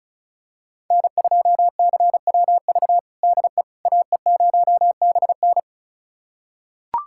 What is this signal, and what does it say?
Morse code, heard as a keyed tone, about 34 wpm, 700 Hz. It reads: N2CWV DE AE0BN
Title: USA Callsign DE Callsign - 5 Characters - 35wpm